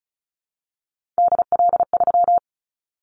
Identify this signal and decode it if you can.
Morse code, heard as a keyed tone, about 36 wpm, 700 Hz. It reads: DL3
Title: Callsign Prefixes for Top 50 DXCC - 35wpm